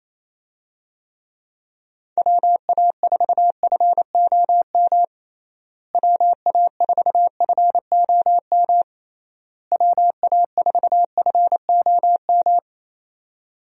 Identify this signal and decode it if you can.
Morse code, heard as a keyed tone, about 27 wpm, 700 Hz. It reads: WA4FOM WA4FOM WA4FOM
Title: USA Call Signs - 6 Characters - Repeated 3x - 28wpm